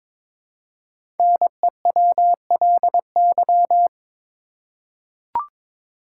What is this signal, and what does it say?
Morse code, heard as a keyed tone, about 23 wpm, 700 Hz. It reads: NEWLY
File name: 5 Letter Words - First 3 Letters Given - 22wpm